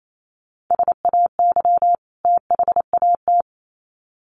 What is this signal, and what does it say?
Morse code, heard as a keyed tone, about 28 wpm, 700 Hz. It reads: SAY THAT
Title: Sets of 2 Words 28wpm